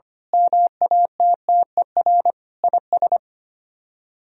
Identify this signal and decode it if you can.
Morse code, heard as a keyed tone, about 24 wpm, 700 Hz. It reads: MATTER IS
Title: Speed-Racing - Sets of 5 Words 25wpm